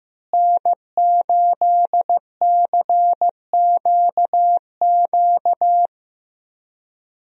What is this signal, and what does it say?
Morse code, heard as a keyed tone, about 15 wpm, 700 Hz. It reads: N8CQQ
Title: USA Call Signs - 5 Characters - Repeated 3x - 15wpm